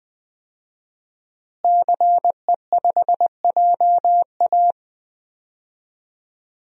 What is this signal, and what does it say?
Morse code, heard as a keyed tone, about 20 wpm, 700 Hz. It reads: CE5JA
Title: DXCC-100 Call Signs - 5 Characters - Repeated 3x - 20wpm